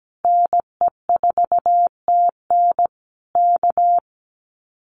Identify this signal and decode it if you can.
Morse code, heard as a keyed tone, about 17 wpm, 700 Hz. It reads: NE4TN K
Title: Common QSO Phrases Encore 17wpm